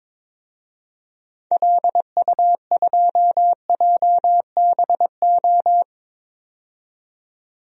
Morse code, heard as a keyed tone, about 22 wpm, 700 Hz. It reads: LU2JBO
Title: DXCC-100 Call Signs - 6 Characters - 22wpm